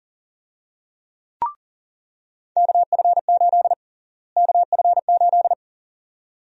Morse code, heard as a keyed tone, about 40 wpm, 700 Hz. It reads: KF8 KF8
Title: USA Call Sign Prefixes - Repeated 2x - 40wpm